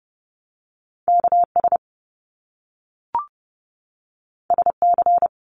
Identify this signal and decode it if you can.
Morse code, heard as a keyed tone, about 30 wpm, 700 Hz. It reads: KS SC
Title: US States 30wpm